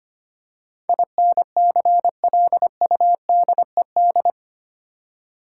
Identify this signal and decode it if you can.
Morse code, heard as a keyed tone, about 25 wpm, 690 Hz. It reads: INCLUDED
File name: 8 Letter Words - First 3 Letters Given - 25wpm